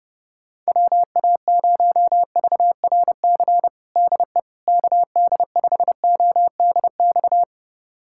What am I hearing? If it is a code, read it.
Morse code, heard as a keyed tone, about 30 wpm, 700 Hz. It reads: WA0VRC DE KD5ODX
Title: USA Callsign DE Callsign - 6 Characters - 30wpm